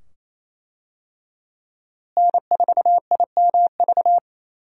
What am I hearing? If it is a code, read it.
Morse code, heard as a keyed tone, about 28 wpm, 700 Hz. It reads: N4IMV
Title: USA Call Signs - 5 Characters - Repeated 2x - 28wpm